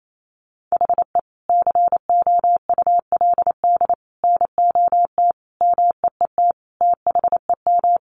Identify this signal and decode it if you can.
Morse code, heard as a keyed tone, about 28 wpm, 700 Hz. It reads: HE COULD NOT MEET THEM
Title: Sentences from Top 1100 Words 28wpm